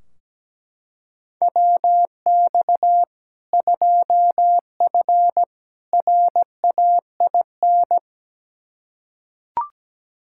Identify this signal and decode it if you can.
Morse code, heard as a keyed tone, about 17 wpm, 700 Hz. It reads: WX 2F RAIN
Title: K1USN SST QSO Practice 17wpm